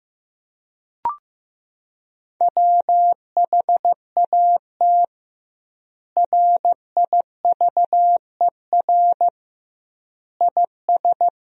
Morse code, heard as a keyed tone, about 15 wpm, 700 Hz. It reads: WHAT RIVER IS
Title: 2x Word-Spacing - Sentences from Top 300 Words - 15wpm